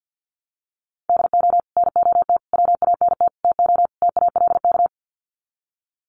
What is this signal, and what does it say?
Morse code, heard as a keyed tone, about 50 wpm, 700 Hz. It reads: DO NOT WANT TO TALK
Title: Sets of 5 Words 50wpm